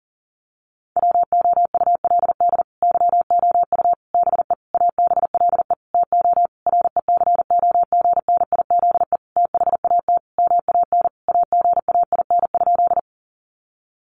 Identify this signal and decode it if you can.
Morse code, heard as a keyed tone, about 40 wpm, 700 Hz. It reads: WOULD YOU BE ABLE TO RECOGNIZE THAT MAN AGAIN?
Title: Sentences from Top 1400 Words 40wpm